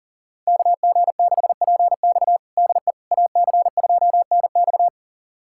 Morse code, heard as a keyed tone, about 40 wpm, 700 Hz. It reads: KG6PX DE AC2NX
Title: USA Callsign DE Callsign - 5 Characters - 40wpm